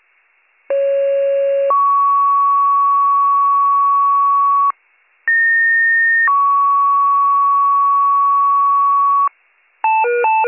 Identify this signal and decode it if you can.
Two-tone page, heard as a one-second tone, then a three-second tone, about 550 then 1120 Hz, about 1850 then 1120 Hz. It is heard from 0.7 to 9.3 s.